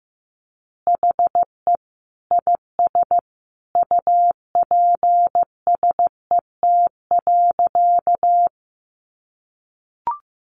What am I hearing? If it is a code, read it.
Morse code, heard as a keyed tone, about 15 wpm, 700 Hz. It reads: HE IS UPSET.
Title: Additional Easy Phrases 15wpm